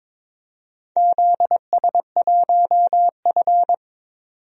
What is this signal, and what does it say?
Morse code, heard as a keyed tone, about 22 wpm, 700 Hz. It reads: ZS1F
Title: DXCC-100 Call Signs - 4 Characters - 22wpm